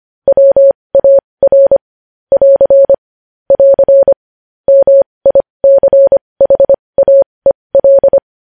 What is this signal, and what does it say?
Morse code, heard as a keyed tone, about 25 wpm, spaced as for 20 wpm, 550 Hz. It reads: WAR <AR> <AR> MICHAEL
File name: podcast / News headlines-1x-25x20